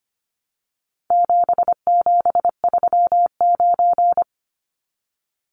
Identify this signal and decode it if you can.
Morse code, heard as a keyed tone, about 25 wpm, 700 Hz. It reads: 7739
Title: Numbers - 4-Digit 25wpm